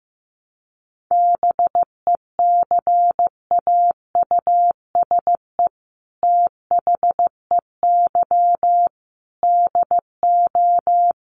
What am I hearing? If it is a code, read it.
Morse code, heard as a keyed tone, about 15 wpm, 700 Hz. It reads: BECAUSE THEY DO
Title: Sets of 5 Words Encore2 15wpm